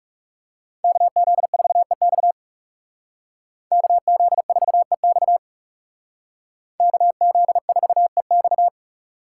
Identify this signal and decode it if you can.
Morse code, heard as a keyed tone, about 36 wpm, 700 Hz. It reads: KZ4EX KZ4EX KZ4EX
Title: Speed Racing - USA 2x2 Call Signs 30wpm